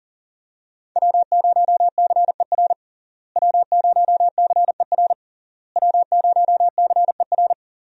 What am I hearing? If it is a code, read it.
Morse code, heard as a keyed tone, about 40 wpm, 700 Hz. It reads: W0CER W0CER W0CER
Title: USA Call Signs - 5 Characters - Repeated 3x - 40wpm